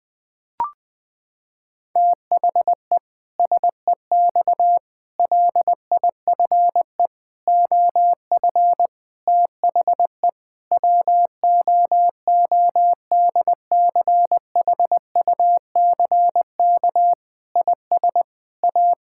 Morse code, heard as a keyed tone, about 20 wpm, 700 Hz. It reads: THE SEX LIFE OF THE WOODCHUCK IS A
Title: Pangram Challenge v2 - 20wpm